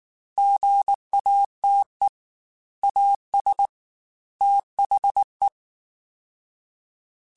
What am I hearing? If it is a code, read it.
Morse code, heard as a keyed tone, about 19 wpm, 800 Hz. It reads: GATE AS THE
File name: Technical Subjects / Technical Subjects-mp3s / OR Gate @18